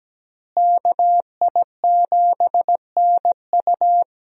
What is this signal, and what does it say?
Morse code, heard as a keyed tone, about 17 wpm, 700 Hz. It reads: KI7NU
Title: USA Call Signs - 5 Characters - Repeated 3x - 17wpm